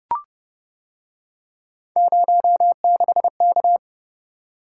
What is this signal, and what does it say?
Morse code, heard as a keyed tone, about 29 wpm, 700 Hz. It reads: 06K